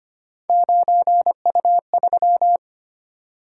Morse code, heard as a keyed tone, about 31 wpm, 700 Hz. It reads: 9U3
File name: Speed-Racing - Three Letter-Number Combinations - 25wpm